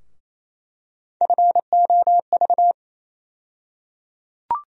Morse code, heard as a keyed tone, about 27 wpm, 700 Hz. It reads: FOV
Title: USA Call Sign Postfixes - 3 Letters 28wpm